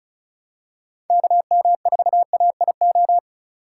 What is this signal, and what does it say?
Morse code, heard as a keyed tone, about 36 wpm, 690 Hz. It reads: KM4AIO
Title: USA Call Signs - 6 Characters - 4x Character Spacing - 35wpm